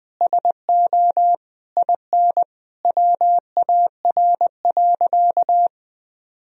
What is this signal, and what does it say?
Morse code, heard as a keyed tone, about 21 wpm, 700 Hz. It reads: SO IN WAR.
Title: Sentences from Top 1000 Words - Review Part 2 - 20wpm